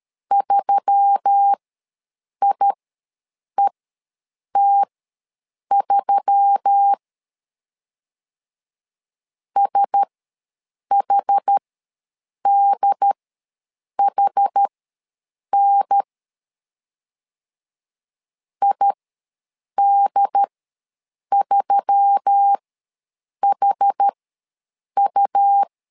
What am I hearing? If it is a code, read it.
Morse code, heard as a keyed tone, about 13 wpm, spaced as for 7 wpm, 800 Hz. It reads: 3IET3 SHDHN ID3HU
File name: LEZIONI / 4 Q1S3 ws 12wpm 2038